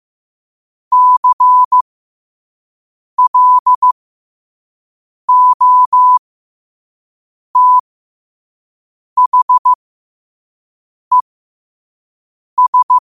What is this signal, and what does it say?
Morse code, heard as a keyed tone, about 15 wpm, spaced as for 5 wpm, 1000 Hz. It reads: CLOTHES
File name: qotdfiles / 5wpm